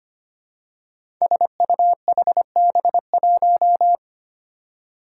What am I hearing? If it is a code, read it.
Morse code, heard as a keyed tone, about 25 wpm, 700 Hz. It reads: SUHB1
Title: Five Challenging Letter-Number Combinations 25wpm